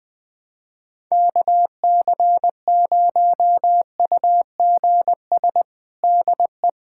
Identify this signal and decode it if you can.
Morse code, heard as a keyed tone, about 20 wpm, 700 Hz. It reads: KC0UGS DE
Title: USA Callsign DE Callsign - 6 Characters - 20wpm